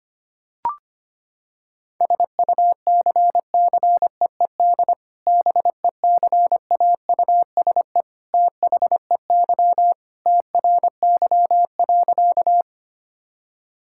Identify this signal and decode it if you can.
Morse code, heard as a keyed tone, about 25 wpm, 700 Hz. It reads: SUCCEED BECAUSE THEY TRY.
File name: practice / 25wpm / Sentences with 2-Word Phrases - 25wpm